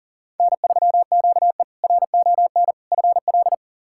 Morse code, heard as a keyed tone, about 37 wpm, 700 Hz. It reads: N3QE RON FL
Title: K1USN SST QSO Practice - 40wpm